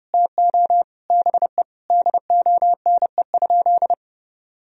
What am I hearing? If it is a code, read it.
Morse code, heard as a keyed tone, about 30 wpm, 700 Hz. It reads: TO BE DONE?
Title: Sentences from Top 1000 Words - Review Part 4 - 30wpm